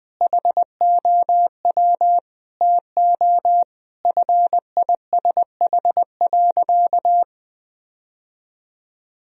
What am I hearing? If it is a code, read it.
Morse code, heard as a keyed tone, about 21 wpm, 700 Hz. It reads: HOW TO FISH.
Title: Sentences from Top 1000 Words - Review Part 1 - 20wpm